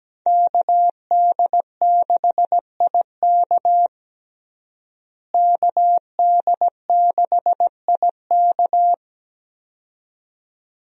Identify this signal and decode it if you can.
Morse code, heard as a keyed tone, about 17 wpm, 700 Hz. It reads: KD6IK KD6IK
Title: USA Call Signs - 5 Characters - Repeated 2x - 17wpm